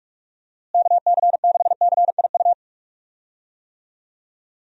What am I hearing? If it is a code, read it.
Morse code, heard as a keyed tone, about 41 wpm, 700 Hz. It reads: KC6CIU
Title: USA Call Signs - 6 Characters - Repeated 3x - 45wpm